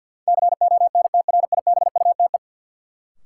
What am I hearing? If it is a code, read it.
Morse code, heard as a keyed tone, about 51 wpm, 700 Hz. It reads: CONTRIBUTE
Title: Sentences from Top 1400 Words - New Word Drill Alt - 50wpm